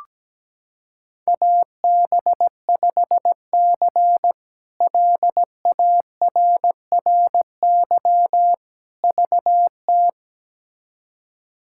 Morse code, heard as a keyed tone, about 17 wpm, 700 Hz. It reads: AB5C LARRY VT
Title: K1USN SST QSO Practice - 17wpm